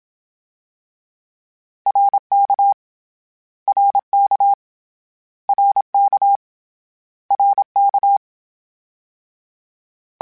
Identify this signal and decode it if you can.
Morse code, heard as a keyed tone, about 27 wpm, 800 Hz. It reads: RK RK RK RK
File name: Short Runs / Short Runs-mp3s / ShortContrasts @25